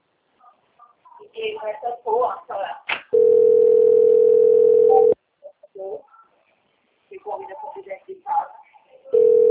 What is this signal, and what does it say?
Ringback tone (North America), 440 + 480 Hz, with speech.
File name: rg-1600-unknown- - - .35089